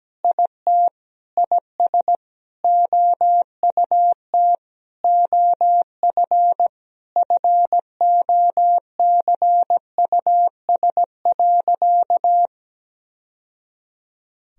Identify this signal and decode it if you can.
Morse code, heard as a keyed tone, about 17 wpm, 700 Hz. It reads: IT IS OUT OF FOCUS.
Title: Sentences from Top 1000 Words - Review Part 4 - 17wpm